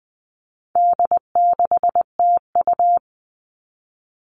Morse code, heard as a keyed tone, about 20 wpm, 700 Hz. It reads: D6TU
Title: All Cut Numbers - 4-Digit 20wpm